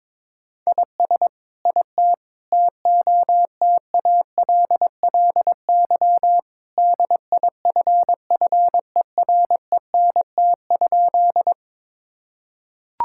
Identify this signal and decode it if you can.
Morse code, heard as a keyed tone, about 22 wpm, 700 Hz. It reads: IS IT TOTALLY DIFFERENT?
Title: Sentences from Top 1600 Words 22wpm